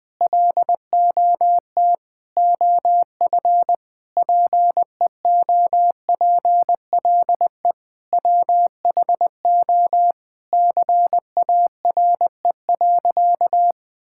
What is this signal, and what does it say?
Morse code, heard as a keyed tone, about 20 wpm, 700 Hz. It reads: LOT OF PEOPLE WHO CARE.
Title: Sentences with 5-Word Phrases - 20wpm